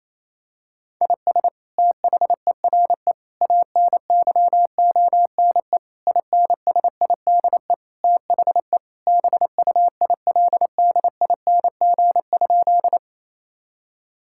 Morse code, heard as a keyed tone, about 28 wpm, 700 Hz. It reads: IS THERE ANYONE INSIDE THE BUILDING?
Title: Sentences from Top 1000 Words - Review Part 6 - 28wpm